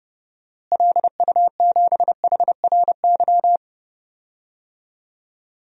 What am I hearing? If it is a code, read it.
Morse code, heard as a keyed tone, about 32 wpm, 700 Hz. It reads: LU7HRY
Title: DXCC-100 Call Signs - 6 Characters - Repeated 3x - 30wpm